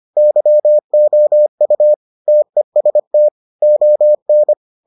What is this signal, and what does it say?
Morse code, heard as a keyed tone, about 25 wpm, 600 Hz. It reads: YOU TEST ON